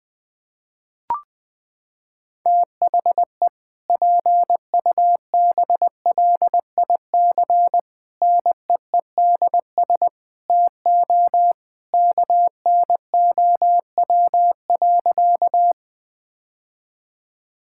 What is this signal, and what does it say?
Morse code, heard as a keyed tone, about 20 wpm, 700 Hz. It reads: THE PUBLIC NEEDS TO KNOW.
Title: Sentences with 2-Word Phrases - 20wpm